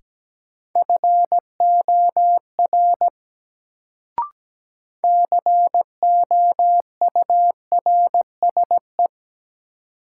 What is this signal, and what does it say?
Morse code, heard as a keyed tone, about 17 wpm, 700 Hz. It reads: FOR COURSE
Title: Sets of 2 Words - New Word Drill - 17wpm